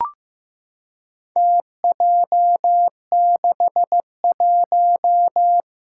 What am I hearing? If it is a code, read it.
Morse code, heard as a keyed tone, about 15 wpm, 700 Hz. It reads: TJ61
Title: Four Challenging Letter-Number Combinations 15wpm